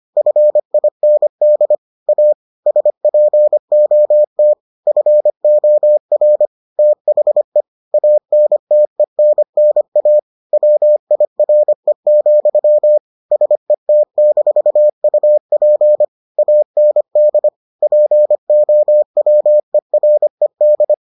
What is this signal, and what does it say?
Morse code, heard as a keyed tone, about 25 wpm, 600 Hz. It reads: FIND A SPOT FOR THE ANTENNA WIRE, SET-UP AND POWERED